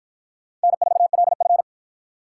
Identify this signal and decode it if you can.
Morse code, heard as a keyed tone, about 48 wpm, 700 Hz. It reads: N4LF